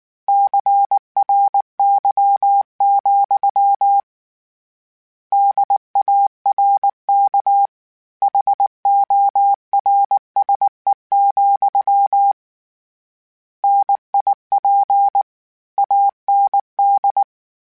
Morse code, heard as a keyed tone, about 19 wpm, 800 Hz. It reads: CRY, DARK HORSE, NIP AND
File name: Misc / Misc-mp3s / Word Pairs @18